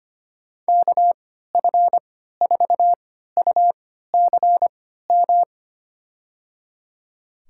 Morse code, heard as a keyed tone, about 25 wpm, spaced as for 14 wpm, 700 Hz. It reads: KF4UCM
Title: USA Call Signs - 6 Characters - 3x Character Spacing - 25wpm